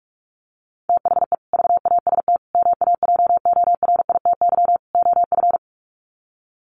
Morse code, heard as a keyed tone, about 44 wpm, 700 Hz. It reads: THE VAST MAJORITY OF
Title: Sets of 4 Words 45wpm